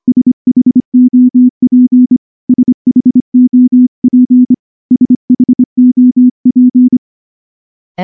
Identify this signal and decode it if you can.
Morse code, heard as a keyed tone, about 24 wpm, 250 Hz. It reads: SHOP SHOP SHOP
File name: morse / morse-training-25WPM